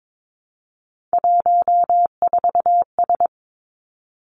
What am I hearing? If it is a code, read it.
Morse code, heard as a keyed tone, about 22 wpm, 700 Hz. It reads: 14S
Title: Three Challenging Letter-Number Combinations 22wpm